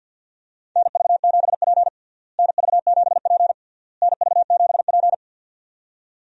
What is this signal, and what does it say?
Morse code, heard as a keyed tone, about 50 wpm, 700 Hz. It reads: NV7P NV7P NV7P
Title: USA Call Signs - 4 Characters - Repeated 3x - 50wpm